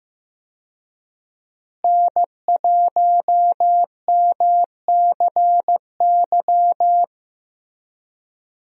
Morse code, heard as a keyed tone, about 15 wpm, 700 Hz. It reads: N1MCY
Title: USA Call Signs - 5 Characters - Repeated 3x - 15wpm